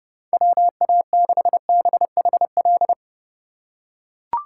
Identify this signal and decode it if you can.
Morse code, heard as a keyed tone, about 31 wpm, 700 Hz. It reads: WA6BHL